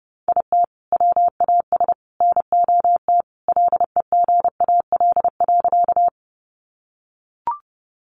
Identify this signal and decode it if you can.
Morse code, heard as a keyed tone, about 30 wpm, 700 Hz. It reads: IT WAS NOT LEGAL.
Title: Sentences from Top 1200 Words 30wpm